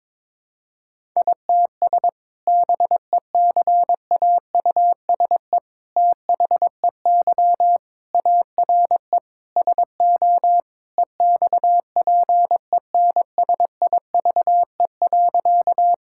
Morse code, heard as a keyed tone, about 21 wpm, 700 Hz. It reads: ITS BECAUSE THEY ARE SO EXPENSIVE.
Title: Sentences from Top 1000 Words - Review Part 6 - 22wpm